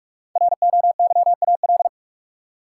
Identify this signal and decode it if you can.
Morse code, heard as a keyed tone, about 46 wpm, 700 Hz. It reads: ROYAL
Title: Five Letter Words - 45wpm